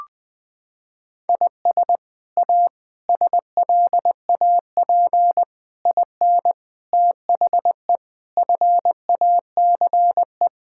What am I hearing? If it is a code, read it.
Morse code, heard as a keyed tone, about 20 wpm, 700 Hz. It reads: IS A SLAP IN THE FACE
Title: Sets of 8 Words - 20wpm